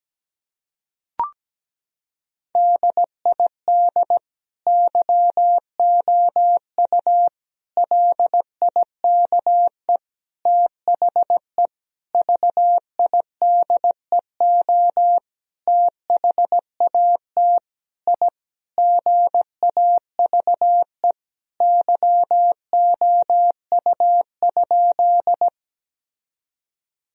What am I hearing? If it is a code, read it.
Morse code, heard as a keyed tone, about 17 wpm, 700 Hz. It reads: DID YOU LIKE THE VIDEO THAT I GAVE YOU?
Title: Sentences from Top 1000 Words - Review Part 4 - 17wpm